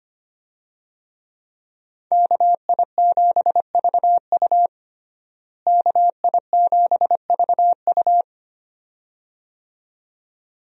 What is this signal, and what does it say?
Morse code, heard as a keyed tone, about 25 wpm, 690 Hz. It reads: KI7VU KI7VU
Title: USA Call Signs - 5 Characters - Repeated 2x - 25wpm